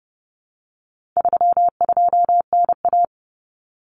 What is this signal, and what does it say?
Morse code, heard as a keyed tone, about 30 wpm, 700 Hz. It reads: 32NA
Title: Common Cut Numbers - 4-Digit 30wpm